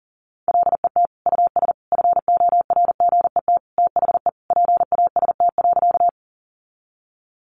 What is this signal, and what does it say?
Morse code, heard as a keyed tone, about 40 wpm, 700 Hz. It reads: LET US FORGET THE PAST.
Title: Sentences from Top 1200 Words 40wpm